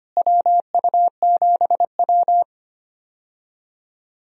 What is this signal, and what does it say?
Morse code, heard as a keyed tone, about 25 wpm, 700 Hz. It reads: WU7W
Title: USA Call Signs - 4 Characters - Repeated 3x - 25wpm